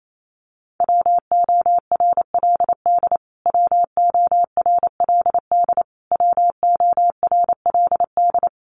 Morse code, heard as a keyed tone, about 28 wpm, 700 Hz. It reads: WORLD WORLD WORLD
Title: Top 200 Words Repeated 28wpm